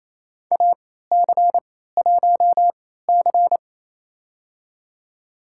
Morse code, heard as a keyed tone, about 28 wpm, spaced as for 16 wpm, 700 Hz. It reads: AC1C
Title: USA Call Signs - 4 Characters - 3x Character Spacing - 28wpm